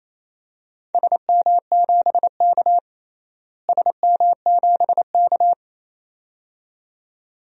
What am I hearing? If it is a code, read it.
Morse code, heard as a keyed tone, about 28 wpm, 700 Hz. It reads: SM7K SM7K
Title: DXCC-100 Call Signs - 4 Characters - Repeated 2x - 28wpm